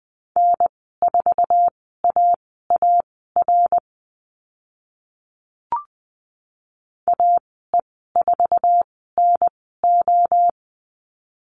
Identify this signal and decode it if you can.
Morse code, heard as a keyed tone, about 20 wpm, spaced as for 14 wpm, 700 Hz. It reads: N4AAR AE4NO
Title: 062 Lesson 11g - Letter h - Call Signs 20wpm